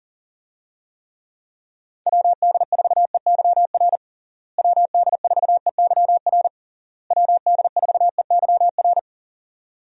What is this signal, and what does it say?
Morse code, heard as a keyed tone, about 40 wpm, 700 Hz. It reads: WD4EYR WD4EYR WD4EYR
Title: USA Call Signs - 6 Characters - Repeated 3x - 40wpm